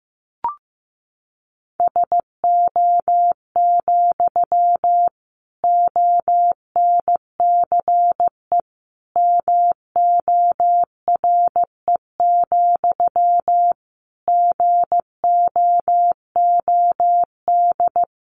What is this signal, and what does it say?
Morse code, heard as a keyed tone, about 15 wpm, 700 Hz. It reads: SO, ONCE MORE, GOOD
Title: Sentences from Top 300 Words 15wpm